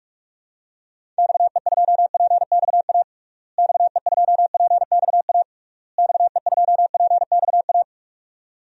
Morse code, heard as a keyed tone, about 41 wpm, 700 Hz. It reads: XE2PXA XE2PXA XE2PXA
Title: DXCC-100 Call Signs - 6 Characters - Repeated 3x - 45wpm